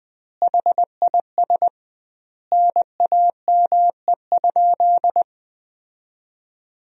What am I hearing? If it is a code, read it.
Morse code, heard as a keyed tone, about 20 wpm, 700 Hz. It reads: HIS NAME?
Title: 2x Word-Spacing - Sentences from Top 200 Words - 20wpm